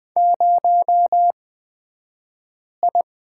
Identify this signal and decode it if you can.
Morse code, heard as a keyed tone, about 20 wpm, spaced as for 5 wpm, 700 Hz. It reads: 0I
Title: 223 Lesson 36d - BT - Two Characters 20wpm